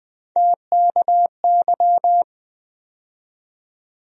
Morse code, heard as a keyed tone, about 20 wpm, 700 Hz. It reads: TKY
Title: USA Call Sign Postfixes - 3 Letters 20wpm